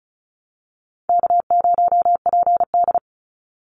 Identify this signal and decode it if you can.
Morse code, heard as a keyed tone, about 34 wpm, 700 Hz. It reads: K0PD
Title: Call Signs - 35wpm